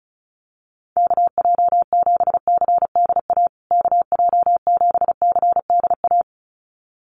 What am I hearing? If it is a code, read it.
Morse code, heard as a keyed tone, about 34 wpm, 700 Hz. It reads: KJ7CDA KJ7CDA
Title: USA Call Signs Repeated 2x 35wpm